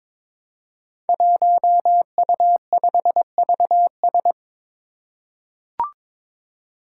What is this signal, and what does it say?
Morse code, heard as a keyed tone, about 23 wpm, 700 Hz. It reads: 1U5VS
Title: Five Challenging Letter-Number Combinations 22wpm